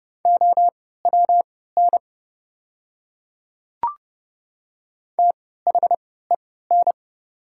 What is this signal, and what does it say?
Morse code, heard as a keyed tone, about 30 wpm, spaced as for 17 wpm, 690 Hz. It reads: OWN THEN
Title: 3x Character Spacing - Top 300 Words - 30wpm